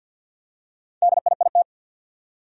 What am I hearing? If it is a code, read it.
Morse code, heard as a keyed tone, about 50 wpm, 700 Hz. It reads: DIIT
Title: Four Challenging Letter-Number Combinations 50wpm